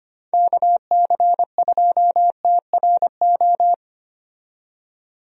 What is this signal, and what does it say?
Morse code, heard as a keyed tone, about 25 wpm, 700 Hz. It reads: KC2TRO
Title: USA Call Signs - 6 Characters - 25wpm